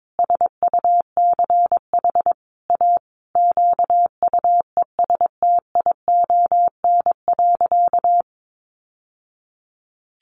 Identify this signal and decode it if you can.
Morse code, heard as a keyed tone, about 22 wpm, 700 Hz. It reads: SUCH A QUESTION.
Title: Sentences from Top 600 Words 22wpm